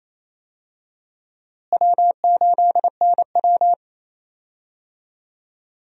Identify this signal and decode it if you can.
Morse code, heard as a keyed tone, about 28 wpm, 700 Hz. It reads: W8NW
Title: USA Call Signs - 4 Characters - Repeated 3x - 28wpm